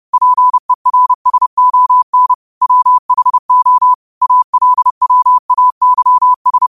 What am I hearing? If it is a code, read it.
Morse code, heard as a keyed tone, about 30 wpm, 1000 Hz. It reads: PERSON WHO ALWAYS
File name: qotdfiles / 30wpm